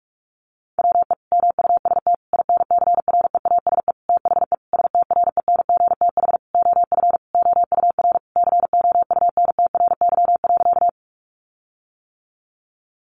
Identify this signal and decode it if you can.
Morse code, heard as a keyed tone, about 44 wpm, 700 Hz. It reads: WE MUST INCREASE THE STRENGTH OF OUR COUNTRY.
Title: Sentences from Top 1300 Words 45wpm